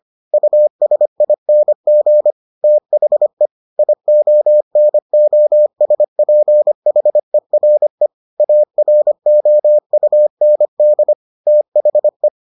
Morse code, heard as a keyed tone, about 25 wpm, 600 Hz. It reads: USING THE IONOSPHERE AROUND THE